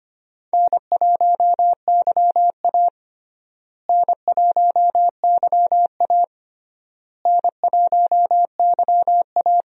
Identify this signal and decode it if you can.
Morse code, heard as a keyed tone, about 25 wpm, 700 Hz. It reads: N1YA N1YA N1YA
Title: USA Call Signs - 4 Characters - Repeated 3x - 25wpm